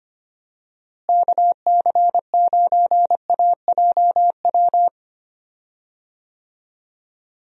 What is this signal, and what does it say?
Morse code, heard as a keyed tone, about 25 wpm, 700 Hz. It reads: KC9AJW
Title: USA Call Signs - 6 Characters - Repeated 3x - 25wpm